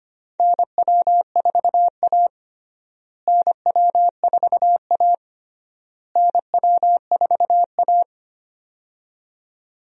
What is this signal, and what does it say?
Morse code, heard as a keyed tone, about 25 wpm, 700 Hz. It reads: NW4A NW4A NW4A
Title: USA Call Signs - 4 Characters - Repeated 3x - 25wpm